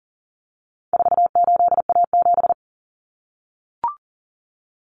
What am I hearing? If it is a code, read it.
Morse code, heard as a keyed tone, about 40 wpm, 700 Hz. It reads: 48A7
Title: Common Cut Numbers - 4-Digit 40wpm